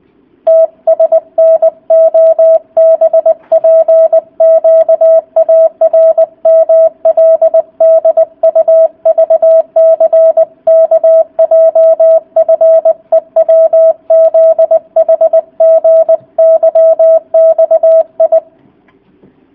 Morse code, heard as a keyed tone, about 20 wpm, 660 Hz. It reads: TSNOBPQARMLDUVCKJFEWZHGYXI